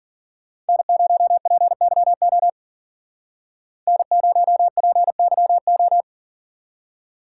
Morse code, heard as a keyed tone, about 52 wpm, 700 Hz. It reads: N0PYO N0PYO